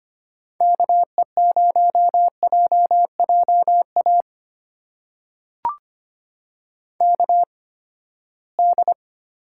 Morse code, heard as a keyed tone, about 25 wpm, spaced as for 17 wpm, 700 Hz. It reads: KE0JJA KD
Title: USA Call Signs - 8x Character Spacing - 25wpm